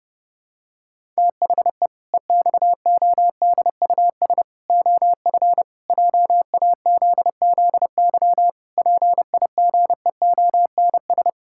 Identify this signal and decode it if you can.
Morse code, heard as a keyed tone, about 32 wpm, 700 Hz. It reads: THE EXODUS OF JAZZY PIGEONS
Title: Pangram Challenge v2 - 30wpm